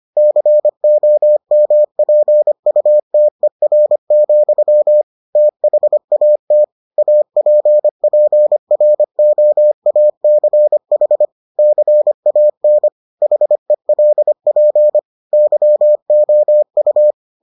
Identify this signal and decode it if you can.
Morse code, heard as a keyed tone, about 25 wpm, 600 Hz. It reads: COMPUTER, THAT APPROACH CAN HELP YOU